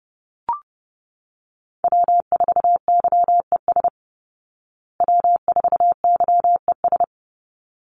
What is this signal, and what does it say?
Morse code, heard as a keyed tone, about 30 wpm, 700 Hz. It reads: W4YES W4YES
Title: Call Signs Repeated 3x - 30wpm